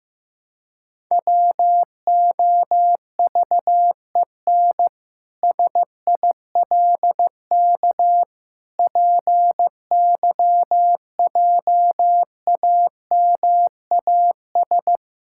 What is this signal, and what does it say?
Morse code, heard as a keyed tone, about 16 wpm, 700 Hz. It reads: WOVEN SILK PYJAMAS
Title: Pangram Challenge v2 - 15wpm